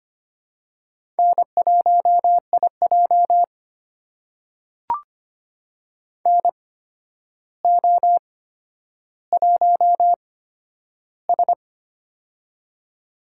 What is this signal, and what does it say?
Morse code, heard as a keyed tone, about 25 wpm, spaced as for 11 wpm, 700 Hz. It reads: N1IJ NO1S
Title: USA Call Signs - 4 Characters - 8x Character Spacing - 25wpm